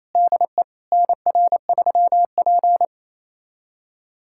Morse code, heard as a keyed tone, about 28 wpm, 700 Hz. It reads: DE NR3P
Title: USA Callsign DE Callsign - 4 Characters - 28wpm